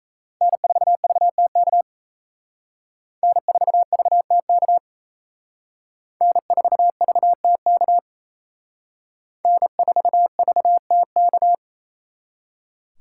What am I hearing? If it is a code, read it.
Morse code, heard as a keyed tone, about 34 wpm, 700 Hz. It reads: N4VTK N4VTK N4VTK N4VTK
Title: Speed Racing - USA 1x3 Call Signs 28wpm